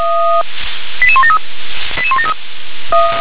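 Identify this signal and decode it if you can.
Five-tone selective call (ZVEI), heard as quick runs of five tones, tones of about 70 ms. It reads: 88173 88173
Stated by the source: Received by SDR.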